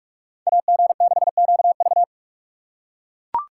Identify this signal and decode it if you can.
Morse code, heard as a keyed tone, about 44 wpm, 700 Hz. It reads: AG6QV